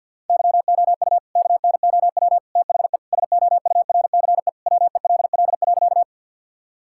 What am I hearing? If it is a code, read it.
Morse code, heard as a keyed tone, about 50 wpm, 700 Hz. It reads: YOU KNOW THE SOURCE WELL.
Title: Sentences from Top 1000 Words - Review Part 5 - 50wpm